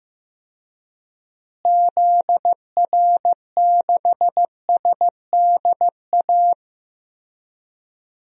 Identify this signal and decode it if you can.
Morse code, heard as a keyed tone, about 15 wpm, 700 Hz. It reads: ZR6SDA
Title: DXCC-100 Call Signs - 6 Characters - Repeated 3x - 15wpm